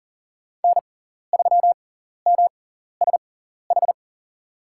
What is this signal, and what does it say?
Morse code, heard as a keyed tone, about 40 wpm, spaced as for 14 wpm, 700 Hz. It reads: N3MSH
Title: USA Call Signs - 5 Characters - 6x Character Spacing - 40wpm